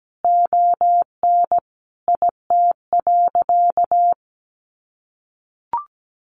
Morse code, heard as a keyed tone, about 17 wpm, 700 Hz. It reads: ON IT.
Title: Sentences from Top 1300 Words 17wpm